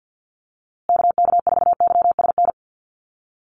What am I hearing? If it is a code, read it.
Morse code, heard as a keyed tone, about 50 wpm, 700 Hz. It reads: KK4YSN
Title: USA Call Signs Repeated 2x 50wpm